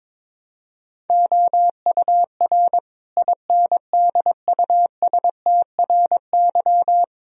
Morse code, heard as a keyed tone, about 22 wpm, 700 Hz. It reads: OUR INDUSTRY
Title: Sentences from Top 1000 Words - Review Part 4 - 22wpm